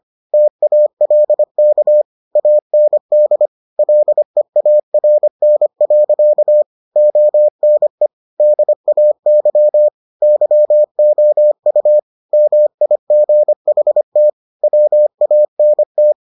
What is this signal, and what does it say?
Morse code, heard as a keyed tone, about 25 wpm, 600 Hz. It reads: TALK AND LEARN. ONE DAY YOU MIGHT WANT